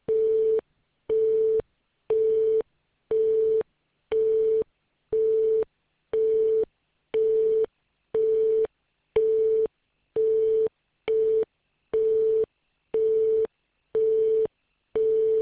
Busy tone (France), 440 Hz.